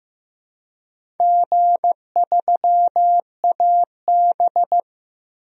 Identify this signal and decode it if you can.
Morse code, heard as a keyed tone, about 15 wpm, 700 Hz. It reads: G3AB
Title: DXCC-100 Call Signs - 4 Characters - Repeated 3x - 15wpm